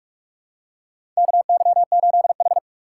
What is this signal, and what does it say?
Morse code, heard as a keyed tone, about 46 wpm, 700 Hz. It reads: KY8H
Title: USA Call Signs - 4 Characters - 2x Character Spacing - 45wpm